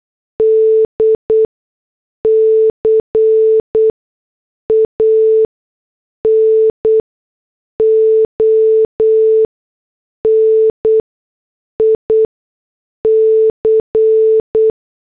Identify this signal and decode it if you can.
Morse code, heard as a keyed tone, about 8 wpm, spaced as for 6 wpm, 440 Hz. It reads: D CANONIC
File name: morse / headlines